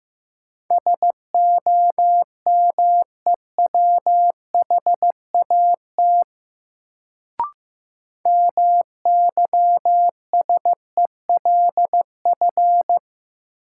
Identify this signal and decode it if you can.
Morse code, heard as a keyed tone, about 15 wpm, 700 Hz. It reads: SOMEWHAT MYSELF
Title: Sentences from Top 1300 Words - New Word Drill Alt - 15wpm